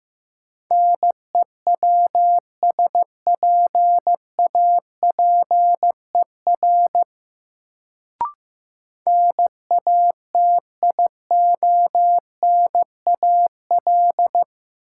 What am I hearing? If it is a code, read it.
Morse code, heard as a keyed tone, about 15 wpm, 700 Hz. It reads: NEWSPAPER NATIONAL
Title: Sentences from Top 1000 Words - Review Part 4 - New Word Drill - 15wpm